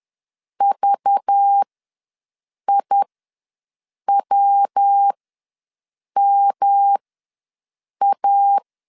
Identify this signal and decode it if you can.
Morse code, heard as a keyed tone, about 10 wpm, spaced as for 6 wpm, 800 Hz. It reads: VIWMA
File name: LEZIONI / 5 LBVC / 5 LBV ws 10wpm 3645